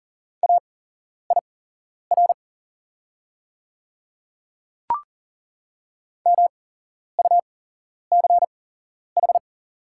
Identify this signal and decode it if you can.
Morse code, heard as a keyed tone, about 38 wpm, spaced as for 11 wpm, 700 Hz. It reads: AIR MUCH
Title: 8x Character Spacing - Top 300 Words - 40wpm